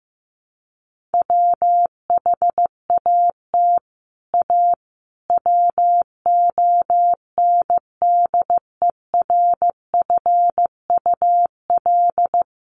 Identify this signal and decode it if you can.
Morse code, heard as a keyed tone, about 15 wpm, 700 Hz. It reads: WHAT A WONDERFUL
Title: Sentences from Top 1100 Words 15wpm